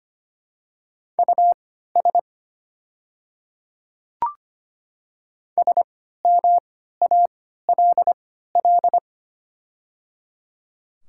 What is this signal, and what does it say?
Morse code, heard as a keyed tone, about 25 wpm, spaced as for 14 wpm, 700 Hz. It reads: US SMALL
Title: 3x Character Spacing - Top 200 Words - 25wpm